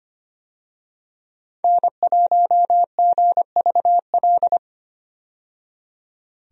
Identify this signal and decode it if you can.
Morse code, heard as a keyed tone, about 25 wpm, 700 Hz. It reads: N1GVL
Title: USA Call Signs - 5 Characters - 25wpm